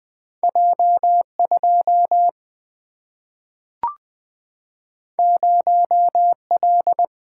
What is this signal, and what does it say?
Morse code, heard as a keyed tone, about 20 wpm, 700 Hz. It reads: J 2 0 L
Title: Single-Letter Single-Number 20wpm